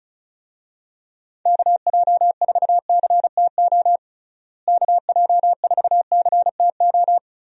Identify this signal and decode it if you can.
Morse code, heard as a keyed tone, about 34 wpm, 700 Hz. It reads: KJ4CTO KJ4CTO
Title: USA Call Signs - 6 Characters - Repeated 2x - 35wpm